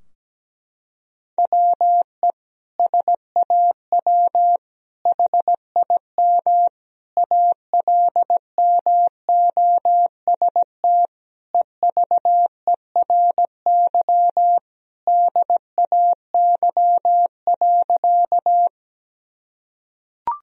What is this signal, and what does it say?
Morse code, heard as a keyed tone, about 17 wpm, 700 Hz. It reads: WE SAW HIM ALMOST EVERY DAY.
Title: Sentences from Top 1000 Words - Review Part 4 - 17wpm